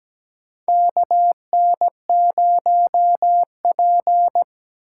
Morse code, heard as a keyed tone, about 17 wpm, 700 Hz. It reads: KN0P
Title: USA Call Signs - 8x Character Spacing - 17wpm